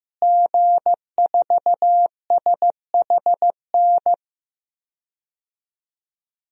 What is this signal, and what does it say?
Morse code, heard as a keyed tone, about 15 wpm, 700 Hz. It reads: G4SHN
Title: DXCC-100 Call Signs - 5 Characters - Repeated 2x - 15wpm